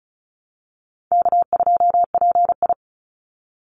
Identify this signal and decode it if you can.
Morse code, heard as a keyed tone, about 36 wpm, 700 Hz. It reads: K2PI
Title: Call Signs - 35wpm